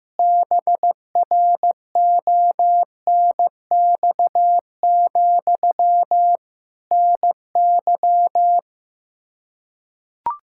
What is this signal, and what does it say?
Morse code, heard as a keyed tone, about 15 wpm, 700 Hz. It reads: BRONX, NY
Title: K1USN SST QSO Practice 15wpm